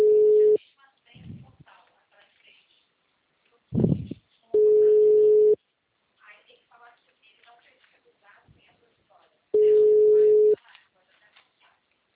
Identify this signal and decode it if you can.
Ringback tone (Europe, 425 Hz), with speech.